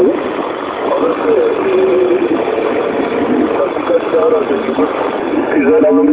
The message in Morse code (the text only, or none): none